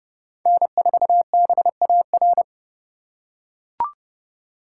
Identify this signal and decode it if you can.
Morse code, heard as a keyed tone, about 32 wpm, 700 Hz. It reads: N4BAR
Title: USA Call Signs - 5 Characters - 3x Character Spacing - 30wpm